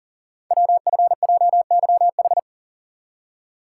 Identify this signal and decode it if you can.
Morse code, heard as a keyed tone, about 42 wpm, 700 Hz. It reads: WFJYH
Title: Code Groups - 5 Letters - 40wpm